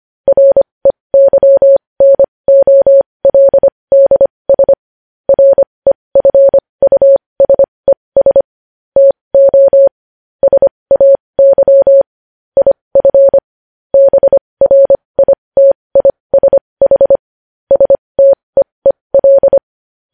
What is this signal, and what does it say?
Morse code, heard as a keyed tone, about 25 wpm, spaced as for 20 wpm, 550 Hz. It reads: REYNOLDS REFUSES TO SAY IF BRITISH STEEL
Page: news-headlines-in-morse-code-at-20-wpm